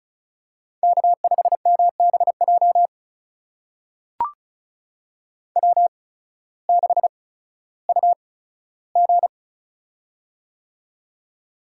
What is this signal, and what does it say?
Morse code, heard as a keyed tone, about 36 wpm, spaced as for 16 wpm, 700 Hz. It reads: K5MBJ W6UG
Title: USA Call Signs - 8x Character Spacing - 35wpm